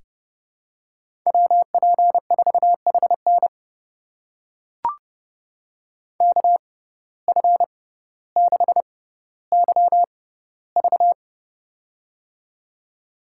Morse code, heard as a keyed tone, about 29 wpm, spaced as for 15 wpm, 700 Hz. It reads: WP4HN KF6YV
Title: USA Call Signs - 5 Characters - 6x Character Spacing - 30wpm